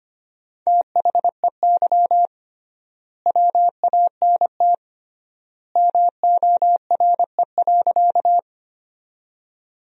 Morse code, heard as a keyed tone, about 25 wpm, 700 Hz. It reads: THEY WANT MORE.
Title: 3x Word-Spacing - Sentences from Top 200 Words - 25wpm